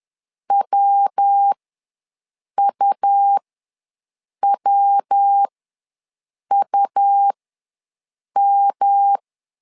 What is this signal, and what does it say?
Morse code, heard as a keyed tone, about 10 wpm, spaced as for 6 wpm, 800 Hz. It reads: WUWUM
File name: LEZIONI / 3 MWRO / 3 MW ws 10wpm 2213